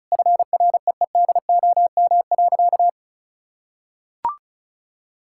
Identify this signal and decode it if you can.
Morse code, heard as a keyed tone, about 36 wpm, 700 Hz. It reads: FREEDOM.
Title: Sentences from Top 1000 Words - Review Part 5 - 35wpm